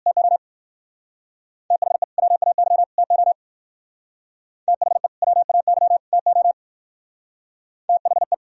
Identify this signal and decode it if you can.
Morse code, heard as a keyed tone, about 54 wpm, 700 Hz. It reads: TO THE WAY TO THE WAY TO THE
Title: Speed-Racing - Sets of 3 Words 45wpm